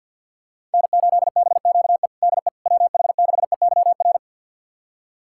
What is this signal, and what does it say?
Morse code, heard as a keyed tone, about 53 wpm, 700 Hz. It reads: N8BQE DE WH6EYR
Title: USA Callsign DE Callsign 50wpm